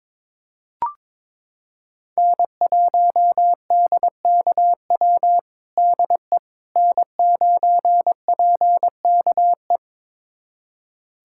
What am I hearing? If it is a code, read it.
Morse code, heard as a keyed tone, about 21 wpm, 700 Hz. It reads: N1DKW DE N9PKE
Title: USA Callsign DE Callsign - 5 Characters - 22wpm